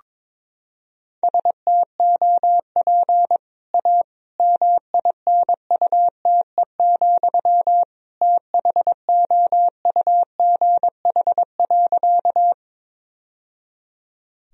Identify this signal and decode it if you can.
Morse code, heard as a keyed tone, about 22 wpm, 700 Hz. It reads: STOP A MINUTE, THOUGH.
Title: Sentences from Top 1000 Words - Review Part 2 - 22wpm